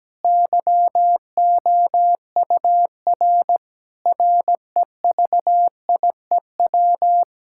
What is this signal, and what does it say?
Morse code, heard as a keyed tone, about 17 wpm, 700 Hz. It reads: YOUR REVIEW
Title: Sentences from Top 1000 Words - Review Part 6 - 17wpm